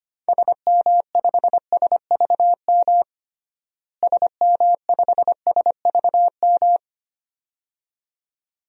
Morse code, heard as a keyed tone, about 25 wpm, 700 Hz. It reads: SM5SVM SM5SVM
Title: DXCC-100 Call Signs - 6 Characters - Repeated 2x - 25wpm